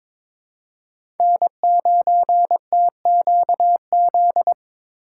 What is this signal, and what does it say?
Morse code, heard as a keyed tone, about 23 wpm, 690 Hz. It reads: N9TQZ
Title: USA Call Signs - 5 Characters - 2x Character Spacing - 22wpm